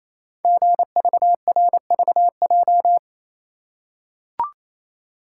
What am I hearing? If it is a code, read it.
Morse code, heard as a keyed tone, about 28 wpm, 690 Hz. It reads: GVRVJ
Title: Code Groups - 5 Letter:Numbers - 28wpm